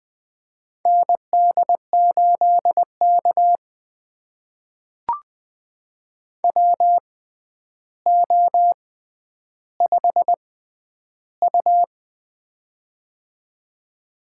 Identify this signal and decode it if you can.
Morse code, heard as a keyed tone, about 20 wpm, spaced as for 10 wpm, 700 Hz. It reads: ND8K WO5U
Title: USA Call Signs - 4 Characters - 6x Character Spacing - 20wpm